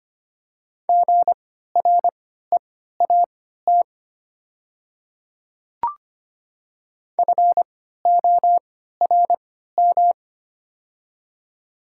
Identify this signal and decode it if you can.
Morse code, heard as a keyed tone, about 25 wpm, spaced as for 14 wpm, 700 Hz. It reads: GREAT FORM
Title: 3x Character Spacing - Top 300 Words - 25wpm